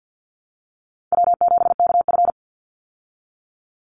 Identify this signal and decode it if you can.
Morse code, heard as a keyed tone, about 50 wpm, 700 Hz. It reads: W7KF
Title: Easy Call Signs Repeated 2x - 50wpm